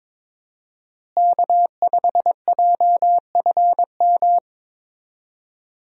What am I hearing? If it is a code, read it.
Morse code, heard as a keyed tone, about 22 wpm, 700 Hz. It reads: K5JFM
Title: USA Call Signs - 5 Characters - Repeated 3x - 22wpm